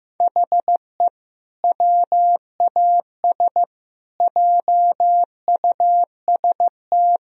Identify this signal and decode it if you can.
Morse code, heard as a keyed tone, about 15 wpm, 700 Hz. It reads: HE WAS JUST
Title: Sentences from Top 1000 Words - Review Part 2 - 15wpm